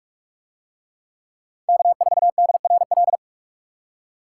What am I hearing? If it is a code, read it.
Morse code, heard as a keyed tone, about 44 wpm, 700 Hz. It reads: K4DRL